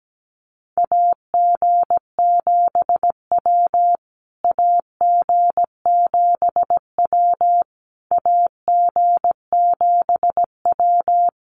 Morse code, heard as a keyed tone, about 17 wpm, 700 Hz. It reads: AG7W AG7W AG7W
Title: USA Call Signs Repeated 3x 17wpm